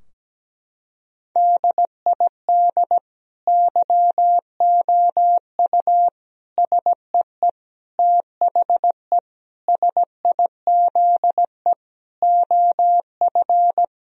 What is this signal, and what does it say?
Morse code, heard as a keyed tone, about 17 wpm, 700 Hz. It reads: DID YOU SEE THE SIZE OF
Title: Sentences from Top 1000 Words - Review Part 2 - 17wpm